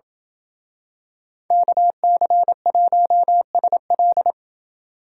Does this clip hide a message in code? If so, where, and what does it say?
Yes: Morse code from 1.5 to 4.3 s, heard as a keyed tone, about 28 wpm, 700 Hz. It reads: KC1SL